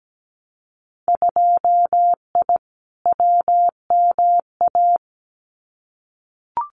Morse code, heard as a keyed tone, about 17 wpm, 700 Hz. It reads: 2I WMA
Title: Winter Field Day Exchanges Encore 17wpm